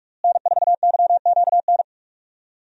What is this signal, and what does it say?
Morse code, heard as a keyed tone, about 46 wpm, 700 Hz. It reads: N4YQN
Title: USA Call Signs - 5 Characters - Repeated 3x - 45wpm